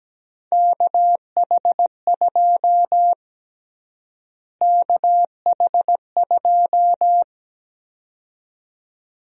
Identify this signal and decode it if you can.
Morse code, heard as a keyed tone, about 17 wpm, 700 Hz. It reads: KH2 KH2
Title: USA Call Sign Prefixes - Repeated 2x - 17wpm